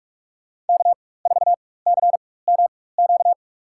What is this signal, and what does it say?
Morse code, heard as a keyed tone, about 46 wpm, spaced as for 21 wpm, 700 Hz. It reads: K4CMQ
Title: USA Call Signs - 5 Characters - 4x Character Spacing - 45wpm